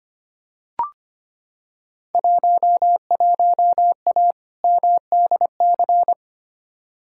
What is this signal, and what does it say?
Morse code, heard as a keyed tone, about 25 wpm, 700 Hz. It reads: 11A MDC
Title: Field Day Exchanges Encore - 25wpm